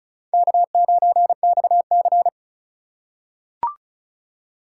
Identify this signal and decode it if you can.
Morse code, heard as a keyed tone, about 34 wpm, 700 Hz. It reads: K9XC
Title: USA Call Signs - 4 Characters - 2x Character Spacing - 35wpm